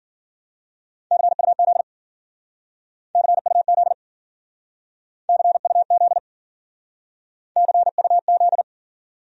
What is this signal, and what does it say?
Morse code, heard as a keyed tone, about 51 wpm, 700 Hz. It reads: CUZ CUZ CUZ CUZ